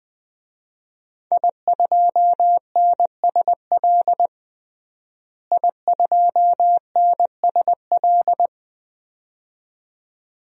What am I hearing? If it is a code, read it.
Morse code, heard as a keyed tone, about 20 wpm, 700 Hz. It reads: I2NSL I2NSL
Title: DXCC-100 Call Signs - 5 Characters - Repeated 2x - 20wpm